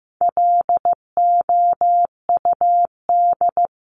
Morse code, heard as a keyed tone, about 15 wpm, 700 Hz. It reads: LOUD
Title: Top 1751-2000 Words 15wpm